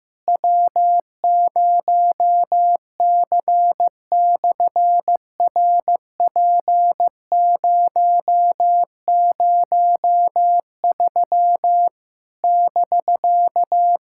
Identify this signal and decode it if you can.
Morse code, heard as a keyed tone, about 15 wpm, 700 Hz. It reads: W0C/RP003 <BK>
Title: SOTA QSO Practice 15wpm